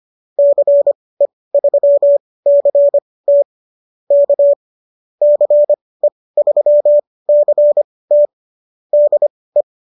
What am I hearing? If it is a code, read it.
Morse code, heard as a keyed tone, about 25 wpm, spaced as for 18 wpm, 590 Hz. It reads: CE3CT K CE3CT DE